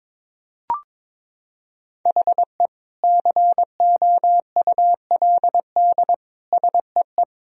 Morse code, heard as a keyed tone, about 22 wpm, 700 Hz. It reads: HE COULD SEE
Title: Sentences from Top 1000 Words - Review Part 1 - 22wpm